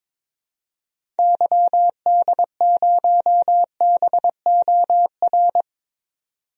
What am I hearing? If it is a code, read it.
Morse code, heard as a keyed tone, about 22 wpm, 700 Hz. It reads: YD0BOR
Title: DXCC-100 Call Signs - 6 Characters - 22wpm